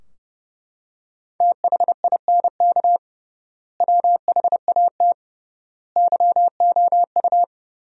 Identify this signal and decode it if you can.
Morse code, heard as a keyed tone, about 29 wpm, 700 Hz. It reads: THINK WHAT YOU
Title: 3x Word-Spacing - Sentences from Top 200 Words - 30wpm